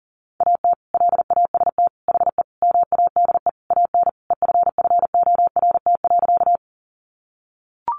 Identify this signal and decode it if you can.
Morse code, heard as a keyed tone, about 38 wpm, 700 Hz. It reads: AT LAST HE MADE AN EFFORT.
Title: Sentences from Top 1200 Words 40wpm